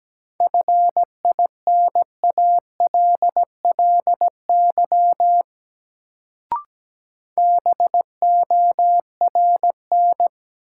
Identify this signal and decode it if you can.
Morse code, heard as a keyed tone, about 17 wpm, 700 Hz. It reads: FINALLY BORN
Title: Sentences from Top 1000 Words - New Word Drill - 17wpm